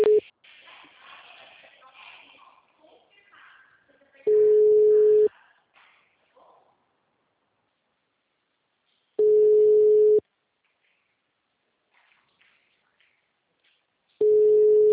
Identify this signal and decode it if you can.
Ringback tone (Europe, 425 Hz), with speech.
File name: out- -2017- - - .54554